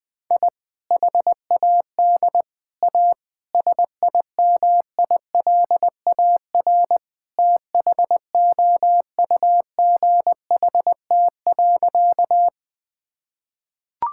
Morse code, heard as a keyed tone, about 20 wpm, 700 Hz. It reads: I HAD A SIMILAR THOUGHT.
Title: Sentences from Top 1000 Words - Review Part 5 - 20wpm